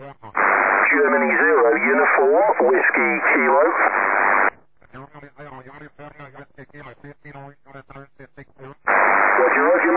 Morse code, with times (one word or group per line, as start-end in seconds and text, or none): none